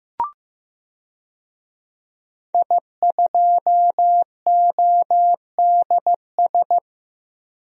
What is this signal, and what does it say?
Morse code, heard as a keyed tone, about 15 wpm, 700 Hz. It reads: I2ODS
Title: DXCC-100 Call Signs - 5 Characters - 15wpm